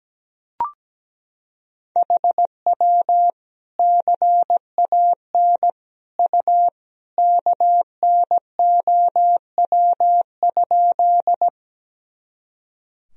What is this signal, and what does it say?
Morse code, heard as a keyed tone, about 17 wpm, 700 Hz. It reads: HW CAN U KNOW?
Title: Sentences from Top 100 Words and Abbreviations 17wpm